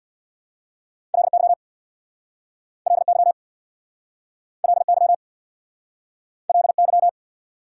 Morse code, heard as a keyed tone, about 69 wpm, 690 Hz. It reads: LY LY LY LY